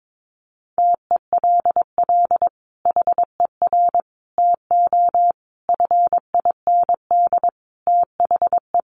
Morse code, heard as a keyed tone, about 22 wpm, 700 Hz. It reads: TELL HER TO FIND THE
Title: Sentences from Top 1200 Words 22wpm